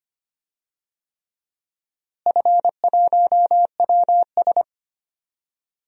Morse code, heard as a keyed tone, about 25 wpm, 700 Hz. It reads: F1WS
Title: DXCC-100 Call Signs - 4 Characters - Repeated 3x - 25wpm